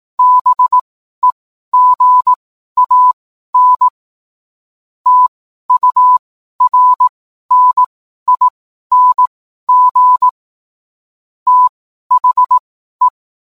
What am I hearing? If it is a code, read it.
Morse code, heard as a keyed tone, about 18 wpm, spaced as for 12 wpm, 1000 Hz. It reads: BEGAN TURNING THE